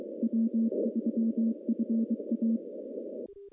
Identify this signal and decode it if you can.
Morse code, heard as a keyed tone, about 23 wpm, 230 Hz. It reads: W3FA
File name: contests / CW-SS-k5tr- -0200